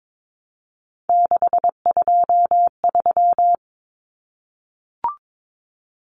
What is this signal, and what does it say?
Morse code, heard as a keyed tone, about 22 wpm, 700 Hz. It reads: 623
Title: Common Cut Numbers - 3-Digit 22wpm